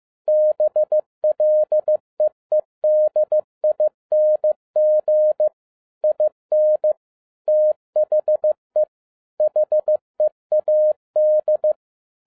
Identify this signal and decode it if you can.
Morse code, heard as a keyed tone, about 15 wpm, 610 Hz. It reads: BLEEDING IN THE HEAD
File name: 2011-07-mp3 / n57 BACK FROM A BIG ACCIDENT 15wpm